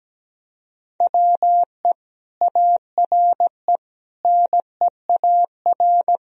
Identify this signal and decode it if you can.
Morse code, heard as a keyed tone, about 17 wpm, 700 Hz. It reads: WE ARE NEAR
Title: Sentences from Top 1000 Words - Review Part 1 - 17wpm